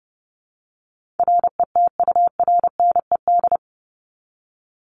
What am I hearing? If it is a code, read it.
Morse code, heard as a keyed tone, about 30 wpm, 700 Hz. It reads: RETURNED
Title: ED Suffix - Top 100 Words with Base Words 30wpm